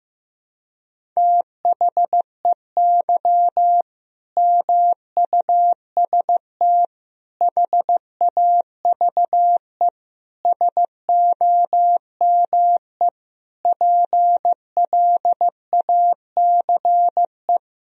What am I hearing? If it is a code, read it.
Morse code, heard as a keyed tone, about 15 wpm, 700 Hz. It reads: THEY MUST HAVE SOME PLACE
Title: Sentences from Top 1000 Words - Review Part 3 - 15wpm